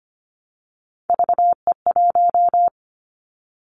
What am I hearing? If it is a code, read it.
Morse code, heard as a keyed tone, about 25 wpm, 700 Hz. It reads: VE1
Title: Callsign Prefixes for Top 50 DXCC - 25wpm